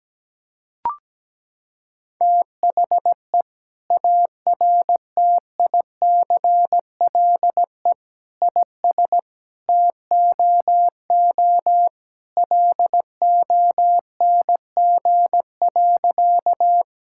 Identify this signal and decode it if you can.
Morse code, heard as a keyed tone, about 17 wpm, 700 Hz. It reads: THE ARTICLE IS TOO LONG.
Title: Sentences from Top 1000 Words - Review Part 5 - 17wpm